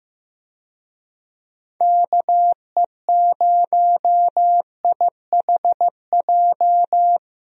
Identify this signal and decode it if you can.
Morse code, heard as a keyed tone, about 15 wpm, 700 Hz. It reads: KE0IHJ
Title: USA Call Signs - 6 Characters - 15wpm